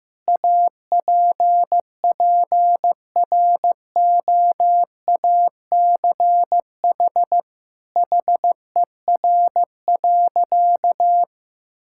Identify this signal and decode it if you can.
Morse code, heard as a keyed tone, about 15 wpm, 700 Hz. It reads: APPROACH HER.
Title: Sentences from Top 1000 Words - Review Part 6 - 15wpm